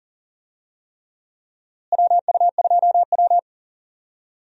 Morse code, heard as a keyed tone, about 40 wpm, 700 Hz. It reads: WU2W
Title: USA Call Signs - 4 Characters - Repeated 3x - 40wpm